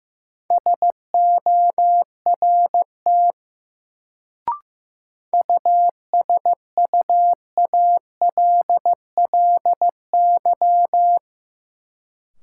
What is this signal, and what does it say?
Morse code, heard as a keyed tone, about 15 wpm, 700 Hz. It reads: SORT USUALLY
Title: Sentences from Top 1000 Words - Review Part 3 - New Word Drill - 15wpm